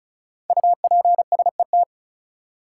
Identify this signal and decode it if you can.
Morse code, heard as a keyed tone, about 34 wpm, 700 Hz. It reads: UPSET
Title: Five Letter Words - 35wpm